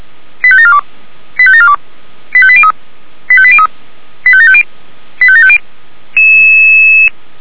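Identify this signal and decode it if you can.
Five-tone selective call (ZVEI), heard as quick runs of five tones, tones of about 70 ms. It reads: 85742 85742 85803 85803 85750 85750
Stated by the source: ZVEI five-tone selective call received on an SDR, from fire brigade Feuerwehr Orlamünde.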